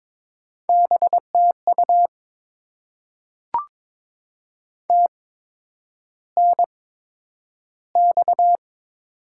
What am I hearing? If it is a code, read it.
Morse code, heard as a keyed tone, about 22 wpm, spaced as for 9 wpm, 700 Hz. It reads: BTU TNX
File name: Top 90 QSO Elements - 8x Character Spacing - 22wpm